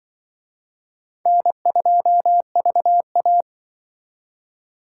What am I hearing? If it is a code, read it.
Morse code, heard as a keyed tone, about 23 wpm, 700 Hz. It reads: N2VA
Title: Speed Racing - USA 1x2 Call Signs 20wpm